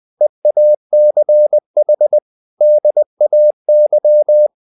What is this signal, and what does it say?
Morse code, heard as a keyed tone, about 20 wpm, 600 Hz. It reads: EACH DAY
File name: the old man and the sea / the old man and the sea part 1 20wpm